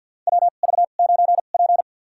Morse code, heard as a keyed tone, about 52 wpm, spaced as for 37 wpm, 700 Hz. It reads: WV9P
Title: USA Call Signs - 4 Characters - 2x Character Spacing - 50wpm